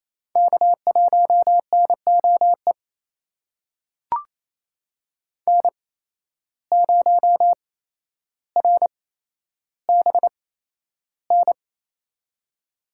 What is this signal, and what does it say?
Morse code, heard as a keyed tone, about 28 wpm, spaced as for 12 wpm, 700 Hz. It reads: K1NOE N0RBN
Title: USA Call Signs - 5 Characters - 8x Character Spacing - 28wpm